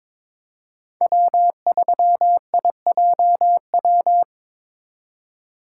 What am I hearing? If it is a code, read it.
Morse code, heard as a keyed tone, about 22 wpm, 700 Hz. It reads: W3IJW
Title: USA Call Signs - 8x Character Spacing - 22wpm